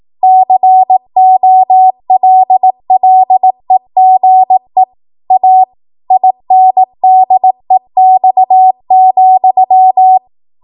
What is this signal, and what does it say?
Morse code, heard as a keyed tone, about 18 wpm, 750 Hz. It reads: COLLEGE A INDEX,
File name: Archive / 18 WPM / 18WPM